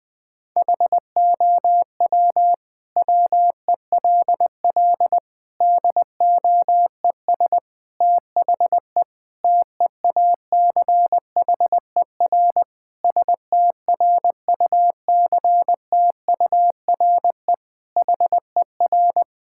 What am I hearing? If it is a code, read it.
Morse code, heard as a keyed tone, about 19 wpm, 700 Hz. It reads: HOW WELL DOES THE TEACHER STRUCTURE HER
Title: Sentences from Top 1000 Words - Review Part 6 - 20wpm